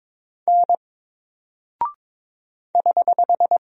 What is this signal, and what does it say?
Morse code, heard as a keyed tone, about 22 wpm, 700 Hz. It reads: N <HH>
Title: Single Character (Letter:Number:Punctuation:Prosign) 22wpm